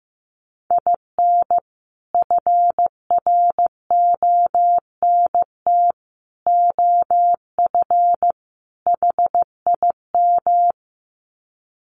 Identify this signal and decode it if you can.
Morse code, heard as a keyed tone, about 15 wpm, 700 Hz. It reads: IN FRONT OF HIM
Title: Sets of 4 Words 15wpm